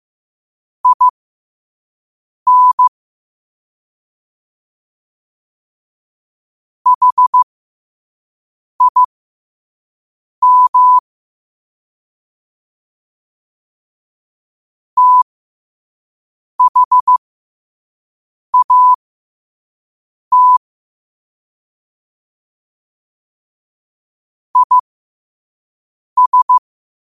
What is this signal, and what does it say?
Morse code, heard as a keyed tone, about 15 wpm, spaced as for 5 wpm, 1000 Hz. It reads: IN HIM THAT IS
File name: qotdfiles / 5wpm